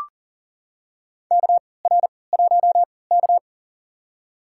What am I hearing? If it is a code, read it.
Morse code, heard as a keyed tone, about 40 wpm, spaced as for 23 wpm, 700 Hz. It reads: KR1K
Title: USA Call Signs - 4 Characters - 3x Character Spacing - 40wpm